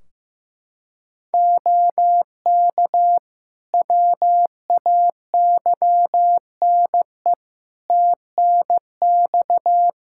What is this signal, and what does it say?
Morse code, heard as a keyed tone, about 15 wpm, 700 Hz. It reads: OK WAYNE TNX
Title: Practice QSO - Exchange 4 - Ending - 15wpm